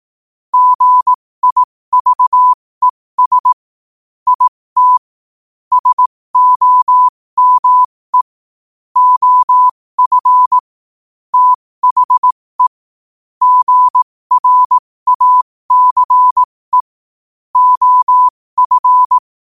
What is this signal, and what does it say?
Morse code, heard as a keyed tone, about 18 wpm, spaced as for 15 wpm, 1000 Hz. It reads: GIVES IT SOME OF THE GRACE OF
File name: qotdfiles / 15wpm